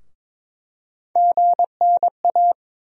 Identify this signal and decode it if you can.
Morse code, heard as a keyed tone, about 22 wpm, 700 Hz. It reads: GNA
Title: USA Call Sign Postfixes - 3 Letters 22wpm